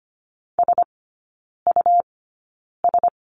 Morse code, heard as a keyed tone, about 25 wpm, spaced as for 9 wpm, 700 Hz. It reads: SUS
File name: Lesson 17e - Letter u - Three Characters 25wpm